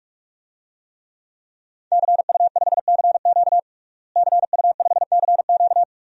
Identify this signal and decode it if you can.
Morse code, heard as a keyed tone, about 46 wpm, 700 Hz. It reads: CU5CQ CU5CQ
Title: DXCC-100 Call Signs - 5 Characters - Repeated 3x - 45wpm